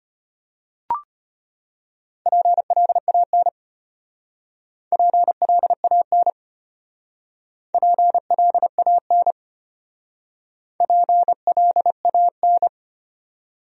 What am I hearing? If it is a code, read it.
Morse code, heard as a keyed tone, about 33 wpm, 700 Hz. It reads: PLAN PLAN PLAN PLAN